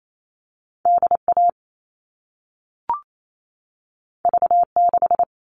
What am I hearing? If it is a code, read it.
Morse code, heard as a keyed tone, about 28 wpm, 700 Hz. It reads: DA V6
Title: All Cut Numbers - 2-Digit 28wpm